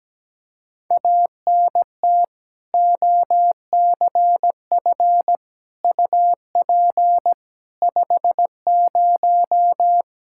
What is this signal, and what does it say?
Morse code, heard as a keyed tone, about 17 wpm, 700 Hz. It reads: ANT OCF UP 50
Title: Practice QSO - Exchange 2 - Info - 17wpm